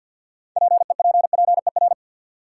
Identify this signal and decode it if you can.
Morse code, heard as a keyed tone, about 52 wpm, 700 Hz. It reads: PEPPER
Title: Sets of 5 Words - New Word Drill - 50wpm